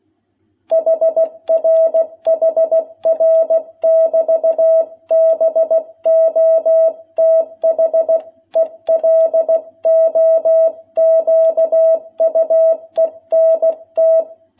Morse code, heard as a keyed tone, about 16 wpm, 660 Hz. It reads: HR HR <BT> BOTH ELOQUENT
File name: jh8cbh / 2021cw